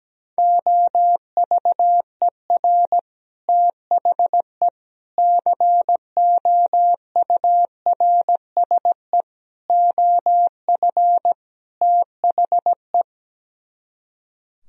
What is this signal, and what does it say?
Morse code, heard as a keyed tone, about 17 wpm, 700 Hz. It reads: OVER THE COURSE OF THE
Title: Speed-Racing - Sets of 5 Words 17wpm